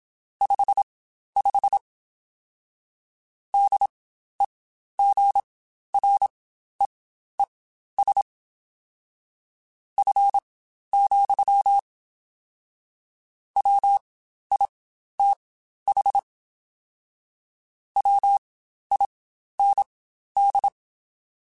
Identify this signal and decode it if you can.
Morse code, heard as a keyed tone, about 27 wpm, spaced as for 12 wpm, 800 Hz. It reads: 55 DEGREES F, WITH WIND
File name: Spaced Characters / Spaced Characters-mp3s / Weather4 24 sp @25